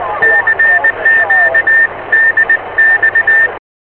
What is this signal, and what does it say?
Morse code, heard as a keyed tone, about 20 wpm, 1800 Hz, with radio noise. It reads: CQ DX